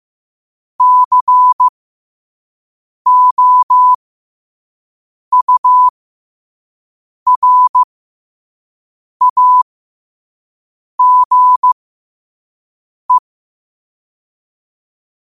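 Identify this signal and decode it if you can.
Morse code, heard as a keyed tone, about 15 wpm, spaced as for 5 wpm, 1000 Hz. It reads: COURAGE
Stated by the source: Morse code sent at 5 wpm.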